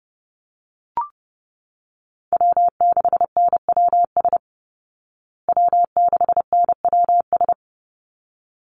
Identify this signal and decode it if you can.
Morse code, heard as a keyed tone, about 30 wpm, 700 Hz. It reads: W6NWS W6NWS
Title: Call Signs Repeated 3x - 30wpm